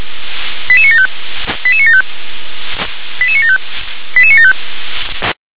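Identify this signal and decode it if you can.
Five-tone selective call (ZVEI), heard as quick runs of five tones, tones of about 70 ms. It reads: 88975 88975 88975 88975
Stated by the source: ZVEI five-tone call received by SDR.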